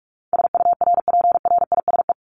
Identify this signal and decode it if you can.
Morse code, heard as a keyed tone, about 44 wpm, 700 Hz. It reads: SURPRISE
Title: Top 1751-2000 Words 45wpm